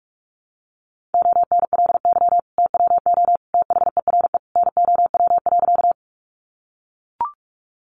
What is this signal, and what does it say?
Morse code, heard as a keyed tone, about 44 wpm, 700 Hz. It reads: ONLY TWO THERE NOW.
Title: Sentences from Top 200 Words 45wpm